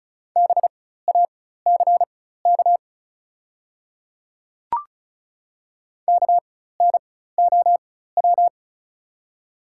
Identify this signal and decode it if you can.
Morse code, heard as a keyed tone, about 36 wpm, spaced as for 16 wpm, 700 Hz. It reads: BACK KNOW
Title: 4x Character Spacing - Top 200 Words - 35wpm